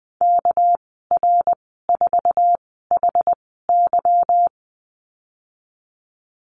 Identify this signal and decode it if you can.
Morse code, heard as a keyed tone, about 20 wpm, 700 Hz. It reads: K R 4 H Y
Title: 231 Lesson 37g - Letter x - Call Signs 20wpm